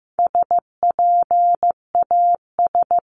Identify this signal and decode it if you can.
Morse code, heard as a keyed tone, about 15 wpm, 700 Hz. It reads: SPAS
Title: CWT Exchange 15wpm